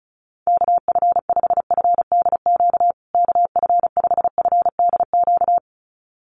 Morse code, heard as a keyed tone, about 34 wpm, 700 Hz. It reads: KF5FDQ KF5FDQ
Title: USA Call Signs Repeated 2x 35wpm